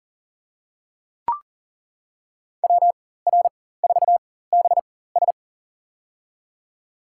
Morse code, heard as a keyed tone, about 40 wpm, spaced as for 19 wpm, 700 Hz. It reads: WR4BS
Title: USA Call Signs - 5 Characters - 4x Character Spacing - 40wpm